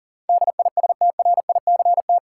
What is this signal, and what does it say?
Morse code, heard as a keyed tone, about 38 wpm, 700 Hz. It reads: DISTRICT
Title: Sentences from Top 1600 Words - New Word Drill Alt - 40wpm